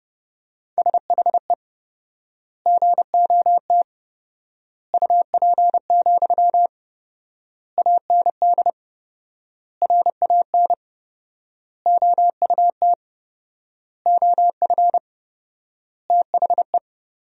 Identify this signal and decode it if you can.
Morse code, heard as a keyed tone, about 28 wpm, 700 Hz. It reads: SHE GOT UP, AND RAN OUT OF THE
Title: 4x Word-Spacing - Sentences from Top 500 Words - 30wpm